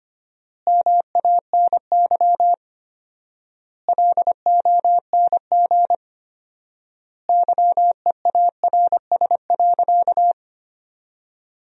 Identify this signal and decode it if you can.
Morse code, heard as a keyed tone, about 25 wpm, 700 Hz. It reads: MANY LONG YEARS.
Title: 4x Word-Spacing - Sentences from Top 200 Words - 25wpm